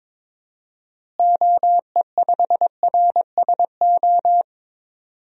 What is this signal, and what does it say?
Morse code, heard as a keyed tone, about 21 wpm, 700 Hz. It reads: OE5RSO
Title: DXCC-100 Call Signs - 6 Characters - Repeated 2x - 22wpm